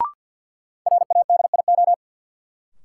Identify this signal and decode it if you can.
Morse code, heard as a keyed tone, about 53 wpm, 700 Hz. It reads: RADIO
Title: Sentences from Top 1000 Words - Review Part 3 - New Word Drill - 50wpm